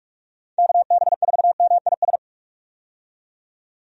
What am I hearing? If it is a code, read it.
Morse code, heard as a keyed tone, about 46 wpm, 700 Hz. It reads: KB4MIS
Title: USA Call Signs - 6 Characters - 45wpm